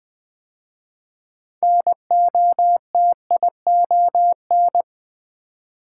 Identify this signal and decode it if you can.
Morse code, heard as a keyed tone, about 20 wpm, 700 Hz. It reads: NOTION
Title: 6 Letter Words - First 3 Letters Given - 20wpm